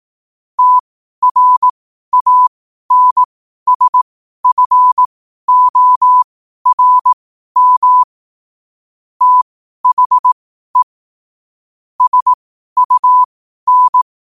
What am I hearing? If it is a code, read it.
Morse code, heard as a keyed tone, about 18 wpm, spaced as for 12 wpm, 1000 Hz. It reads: TRANSFORM THE SUN
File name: qotdfiles / 12wpm